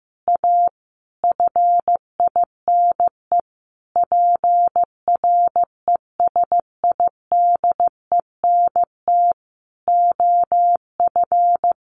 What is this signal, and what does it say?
Morse code, heard as a keyed tone, about 15 wpm, 700 Hz. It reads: A FINE PRESIDENT OF
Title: A Sound Of Thunder 15wpm